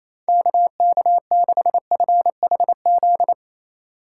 Morse code, heard as a keyed tone, about 36 wpm, 700 Hz. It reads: KK6FHZ